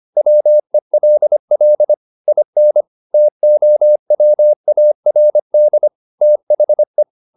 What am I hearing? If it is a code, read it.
Morse code, heard as a keyed tone, about 25 wpm, 600 Hz. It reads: WELL IN TOWARD THE
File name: the old man and the sea part 1 25wpm